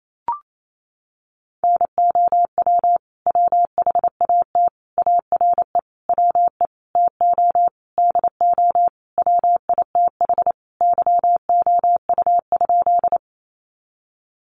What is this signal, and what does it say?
Morse code, heard as a keyed tone, about 28 wpm, 700 Hz. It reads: NOW WHAT ARE WE TO DO WITH YOU?
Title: Sentences from Top 100 Words 28wpm